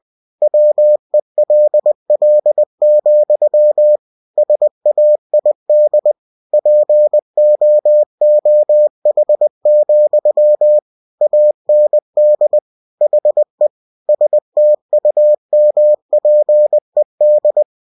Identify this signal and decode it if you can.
Morse code, heard as a keyed tone, about 20 wpm, 600 Hz. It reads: WELL, SAID POOH, AND HE STUMPED
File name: cw-books / winnie-the-pooh / winnie-20wpm-page-0037